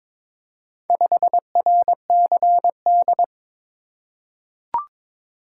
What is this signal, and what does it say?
Morse code, heard as a keyed tone, about 23 wpm, 700 Hz. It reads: 5RCD
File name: practice / 22wpm / Four Letter-Number Combinations - 22wpm